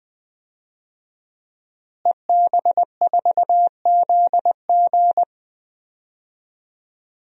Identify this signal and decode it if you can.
Morse code, heard as a keyed tone, about 19 wpm, 700 Hz. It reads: EB4ZG
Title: DXCC-100 Call Signs - 5 Characters - Repeated 3x - 20wpm